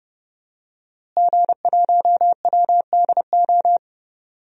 Morse code, heard as a keyed tone, about 30 wpm, 700 Hz. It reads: G1WDO
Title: DXCC-100 Call Signs - 5 Characters - Repeated 2x - 30wpm